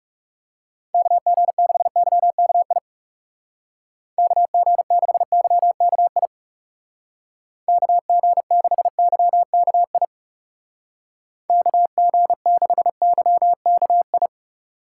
Morse code, heard as a keyed tone, about 38 wpm, 700 Hz. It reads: KG6YKI KG6YKI KG6YKI KG6YKI